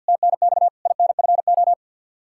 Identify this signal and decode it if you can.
Morse code, heard as a keyed tone, about 52 wpm, 700 Hz. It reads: TNX INFO
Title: Practice QSO - Exchange 3 - More Info - 50wpm